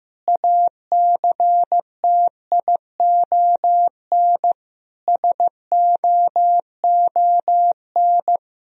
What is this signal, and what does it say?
Morse code, heard as a keyed tone, about 16 wpm, 700 Hz. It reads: ACTION SOON
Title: Sentences from Top 1000 Words - Review Part 4 - 15wpm